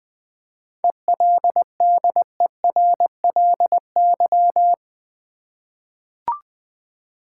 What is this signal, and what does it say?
Morse code, heard as a keyed tone, about 21 wpm, 690 Hz. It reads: ELDERLY
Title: 7 Letter Words - First 3 Letters Given - 20wpm